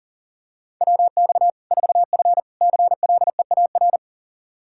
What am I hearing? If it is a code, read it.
Morse code, heard as a keyed tone, about 40 wpm, 700 Hz. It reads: WX 4F CLEAR
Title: Practice QSO - Exchange 2 - Info - 40wpm